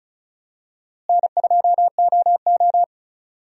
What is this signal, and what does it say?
Morse code, heard as a keyed tone, about 34 wpm, 700 Hz. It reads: N2OO
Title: USA Call Signs - 4 Characters - Repeated 3x - 35wpm